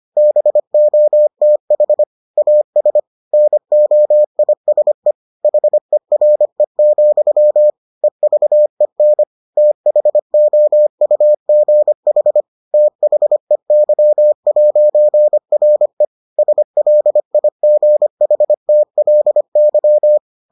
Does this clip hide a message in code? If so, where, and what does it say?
Yes: Morse code from 0.2 to 20.2 s, heard as a keyed tone, about 25 wpm, 600 Hz. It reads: BOTH AS NOISE HERE, EVEN THOUGH THEY'RE SLIGHTLY